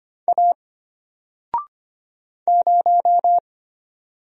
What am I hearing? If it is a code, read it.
Morse code, heard as a keyed tone, about 25 wpm, 700 Hz. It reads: A 0
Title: Single Letter-Number Alt 25wpm